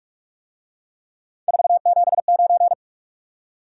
Morse code, heard as a keyed tone, about 43 wpm, 700 Hz. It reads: 479